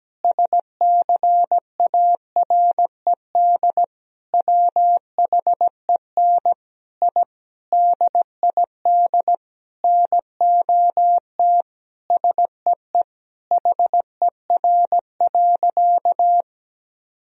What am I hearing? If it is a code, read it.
Morse code, heard as a keyed tone, about 17 wpm, 700 Hz. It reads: SCARED WHEN I DID NOT SEE HER.
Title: Sentences from Top 1000 Words - Review Part 4 - 17wpm